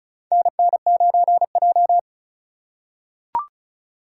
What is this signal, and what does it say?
Morse code, heard as a keyed tone, about 36 wpm, 700 Hz. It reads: NN9J
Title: USA Call Signs - 4 Characters - 2x Character Spacing - 35wpm